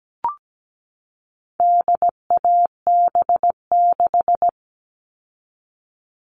Morse code, heard as a keyed tone, about 17 wpm, 700 Hz. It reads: DAB6
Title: All Cut Numbers - 4-Digit 17wpm